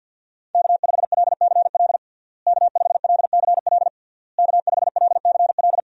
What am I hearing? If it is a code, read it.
Morse code, heard as a keyed tone, about 52 wpm, 700 Hz. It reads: K5LCL K5LCL K5LCL
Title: USA Call Signs - 5 Characters - Repeated 3x - 50wpm